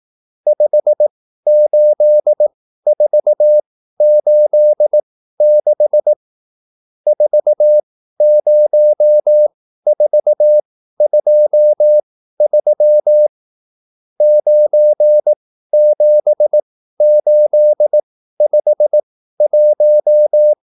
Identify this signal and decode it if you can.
Morse code, heard as a keyed tone, about 18 wpm, spaced as for 13 wpm, 600 Hz. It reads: 58486 40423 97851